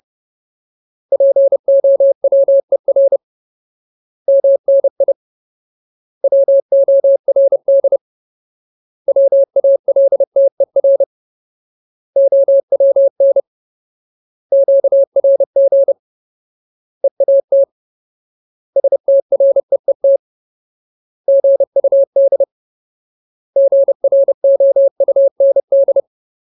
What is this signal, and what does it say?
Morse code, heard as a keyed tone, about 30 wpm, 550 Hz. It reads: POWER MNI WORD WALTER OWN QRG EAT STREET GUD GROUND
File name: Ultimate Word List / 30WPM Ultimate